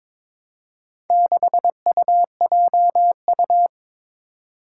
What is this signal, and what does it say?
Morse code, heard as a keyed tone, about 22 wpm, 700 Hz. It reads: 6UJU
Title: Four Challenging Letter-Number Combinations 22wpm